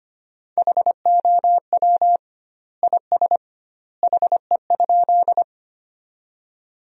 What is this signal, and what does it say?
Morse code, heard as a keyed tone, about 25 wpm, 700 Hz. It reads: HOW IS HE?
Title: 2x Word-Spacing - Sentences from Top 100 Words - 25wpm